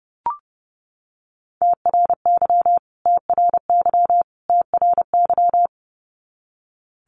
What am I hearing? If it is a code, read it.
Morse code, heard as a keyed tone, about 30 wpm, 700 Hz. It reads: TRY TRY TRY
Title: Top 500 Words Repeated 30wpm